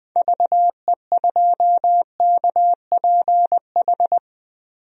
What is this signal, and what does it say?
Morse code, heard as a keyed tone, about 20 wpm, 700 Hz. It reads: VE2KPH
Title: DXCC-100 Call Signs - 6 Characters - Repeated 2x - 20wpm